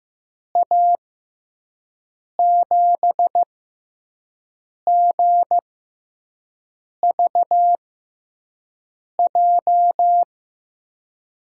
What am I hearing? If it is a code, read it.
Morse code, heard as a keyed tone, about 15 wpm, spaced as for 5 wpm, 700 Hz. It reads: A7GVJ
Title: USA Call Signs - 6 Characters - 6x Character Spacing - 15wpm